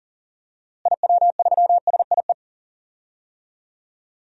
Morse code, heard as a keyed tone, about 39 wpm, 700 Hz. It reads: IW3SIE
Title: DXCC-100 Call Signs - 6 Characters - Repeated 3x - 40wpm